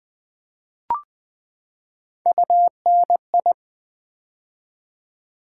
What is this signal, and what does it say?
Morse code, heard as a keyed tone, about 20 wpm, 700 Hz. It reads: UNI
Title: Build A Word - 3 Letters At A Time - 9-Letter Words - 20wpm